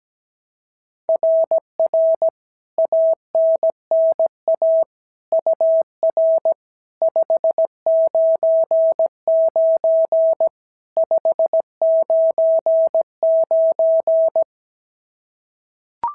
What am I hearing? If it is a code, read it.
Morse code, heard as a keyed tone, about 16 wpm, 650 Hz. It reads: RR ANNA UR 599 599
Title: SOTA QSO Practice 17wpm